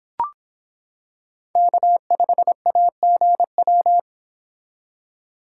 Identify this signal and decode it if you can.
Morse code, heard as a keyed tone, about 26 wpm, 700 Hz. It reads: K5AGW